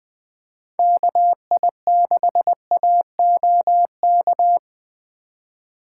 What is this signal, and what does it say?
Morse code, heard as a keyed tone, about 19 wpm, 700 Hz. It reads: KI6AOK